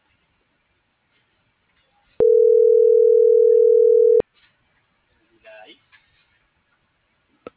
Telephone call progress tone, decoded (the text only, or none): ringback tone